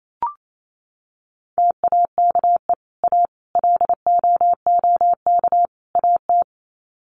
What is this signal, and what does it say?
Morse code, heard as a keyed tone, about 28 wpm, 700 Hz. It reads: TAKE A LOOK AT
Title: Sets of 4 Words 28wpm